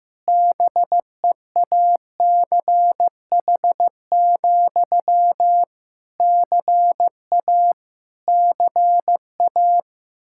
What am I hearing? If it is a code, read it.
Morse code, heard as a keyed tone, about 15 wpm, 700 Hz. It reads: BEACH, CA CA
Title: K1USN SST QSO Practice 15wpm